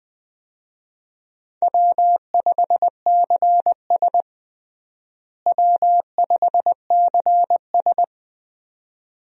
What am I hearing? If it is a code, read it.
Morse code, heard as a keyed tone, about 20 wpm, 700 Hz. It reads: W5CS W5CS
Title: USA Call Signs - 4 Characters - Repeated 2x - 20wpm